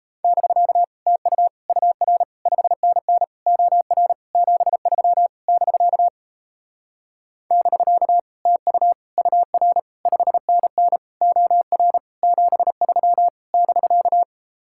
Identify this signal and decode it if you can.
Morse code, heard as a keyed tone, about 37 wpm, 700 Hz. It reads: <BK> TU UR 5NN OR 73 <BK> <BK> TU UR 5NN OR 73 <BK>